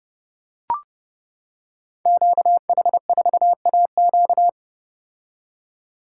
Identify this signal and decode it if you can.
Morse code, heard as a keyed tone, about 31 wpm, 700 Hz. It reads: QH4AQ
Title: Code Groups - 5 Letter:Numbers - 30wpm